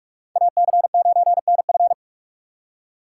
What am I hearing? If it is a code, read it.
Morse code, heard as a keyed tone, about 59 wpm, spaced as for 52 wpm, 700 Hz. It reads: AC9NF
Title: Speed Racing - USA Call Signs 45wpm